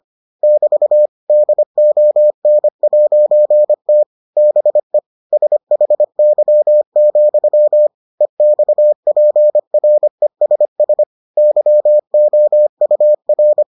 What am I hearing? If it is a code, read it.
Morse code, heard as a keyed tone, about 25 wpm, 600 Hz. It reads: <BT> DON'T BE SHY, EXPRESS YOUR